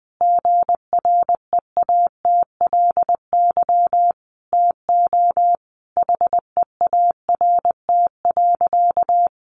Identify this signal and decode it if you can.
Morse code, heard as a keyed tone, about 20 wpm, 700 Hz. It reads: GREATLY TO HEART.
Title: Sentences from Top 1400 Words 20wpm